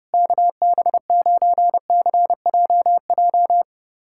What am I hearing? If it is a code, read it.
Morse code, heard as a keyed tone, about 28 wpm, 700 Hz. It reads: KB9CJJ
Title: USA Callsign DE Callsign - 6 Characters - 30wpm